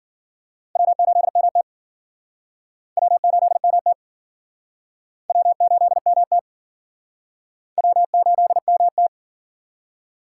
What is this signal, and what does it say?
Morse code, heard as a keyed tone, about 52 wpm, 700 Hz. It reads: W8MT W8MT W8MT W8MT